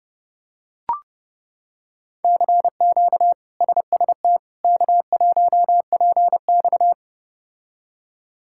Morse code, heard as a keyed tone, about 30 wpm, 700 Hz. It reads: CQ SST K1PX
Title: K1USN SST QSO Practice - 30wpm